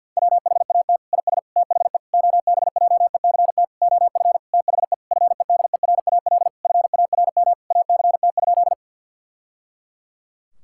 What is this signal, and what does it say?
Morse code, heard as a keyed tone, about 51 wpm, 700 Hz. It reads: WHAT IS THE OBJECT OF THE FEDERAL FARM ACT?
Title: Sentences from Top 1000 Words - Review Part 5 - 50wpm